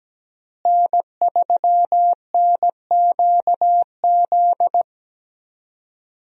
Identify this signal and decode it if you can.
Morse code, heard as a keyed tone, about 17 wpm, 700 Hz. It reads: N3NQZ
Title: USA Call Signs - 5 Characters - Repeated 3x - 17wpm